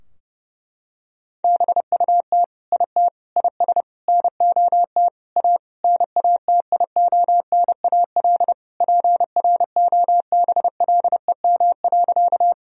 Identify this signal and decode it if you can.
Morse code, heard as a keyed tone, about 32 wpm, 700 Hz. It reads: BUT IT IS NOT A NATIONAL PROBLEM.
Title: Sentences from Top 1000 Words - Review Part 4 - 30wpm